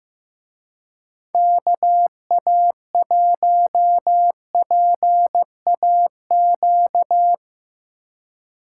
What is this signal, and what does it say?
Morse code, heard as a keyed tone, about 15 wpm, 700 Hz. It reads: KA1PAQ
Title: Speed Racing - USA 2x3 Call Signs 15wpm